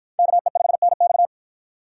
Morse code, heard as a keyed tone, about 50 wpm, 700 Hz. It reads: KE5NX